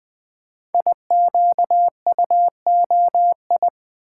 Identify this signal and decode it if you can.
Morse code, heard as a keyed tone, about 20 wpm, 700 Hz. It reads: IQUOI
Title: Code Groups - 5 Letters - 20wpm